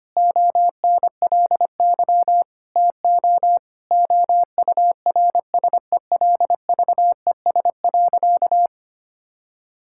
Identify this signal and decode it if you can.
Morse code, heard as a keyed tone, about 25 wpm, 700 Hz. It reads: ONLY TO OURSELVES.
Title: Sentences from Top 1600 Words 25wpm